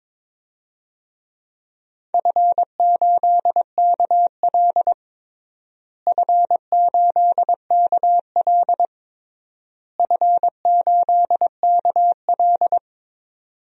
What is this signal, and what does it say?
Morse code, heard as a keyed tone, about 22 wpm, 700 Hz. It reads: F8KL F8KL F8KL
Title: DXCC-100 Call Signs - 4 Characters - Repeated 3x - 22wpm